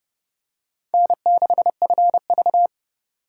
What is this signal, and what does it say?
Morse code, heard as a keyed tone, about 29 wpm, 700 Hz. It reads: N6FV